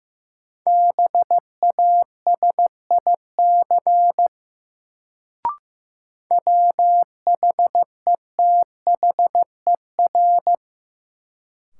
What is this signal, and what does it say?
Morse code, heard as a keyed tone, about 15 wpm, 700 Hz. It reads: BASIC WHETHER
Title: Sentences from Top 1000 Words - Review Part 3 - New Word Drill - 15wpm